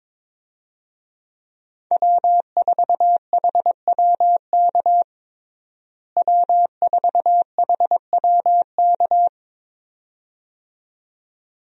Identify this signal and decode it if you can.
Morse code, heard as a keyed tone, about 22 wpm, 700 Hz. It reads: W4HWK W4HWK
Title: USA Call Signs - 5 Characters - Repeated 2x - 22wpm